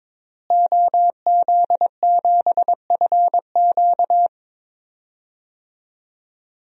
Morse code, heard as a keyed tone, about 21 wpm, 700 Hz. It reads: OZ7FQ
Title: DXCC-100 Call Signs - 5 Characters - Repeated 2x - 22wpm